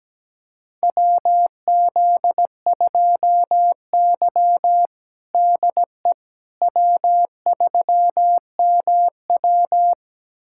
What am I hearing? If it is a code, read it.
Morse code, heard as a keyed tone, about 17 wpm, 700 Hz. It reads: WZ2Y DE W3MW
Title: USA Callsign DE Callsign - 4 Characters - 17wpm